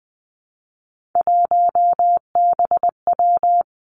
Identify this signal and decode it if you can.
Morse code, heard as a keyed tone, about 19 wpm, 700 Hz. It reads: 1BW
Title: Three Challenging Letter-Number Combinations 20wpm